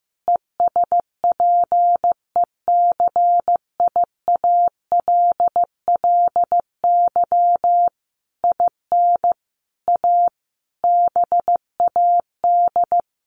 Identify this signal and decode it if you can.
Morse code, heard as a keyed tone, about 15 wpm, 700 Hz. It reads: ESPECIALLY IN A BAD
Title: Sentences from Top 1400 Words 15wpm